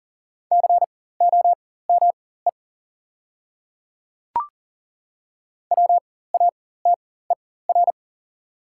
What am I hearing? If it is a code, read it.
Morse code, heard as a keyed tone, about 40 wpm, spaced as for 19 wpm, 700 Hz. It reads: COME WATER
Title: 4x Character Spacing - Top 200 Words - 40wpm